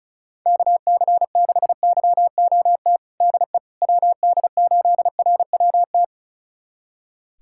Morse code, heard as a keyed tone, about 36 wpm, 700 Hz. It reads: KC6YOT DE WD8RWT
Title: USA Callsign DE Callsign 35wpm